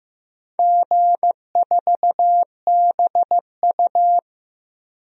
Morse code, heard as a keyed tone, about 15 wpm, 700 Hz. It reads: G4BU
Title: DXCC-100 Call Signs - 4 Characters - Repeated 3x - 15wpm